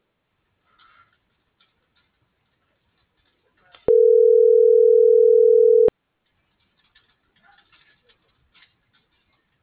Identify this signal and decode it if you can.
Ringback tone (North America), 440 + 480 Hz.